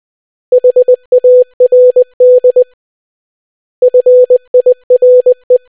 Morse code, heard as a keyed tone, about 20 wpm, 500 Hz. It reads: HARD FIRE